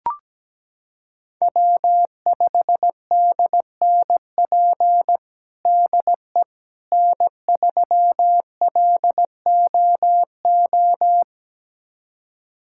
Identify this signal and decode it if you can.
Morse code, heard as a keyed tone, about 17 wpm, 700 Hz. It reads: W5DNP DE N3LOO
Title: USA Callsign DE Callsign - 5 Characters - 17wpm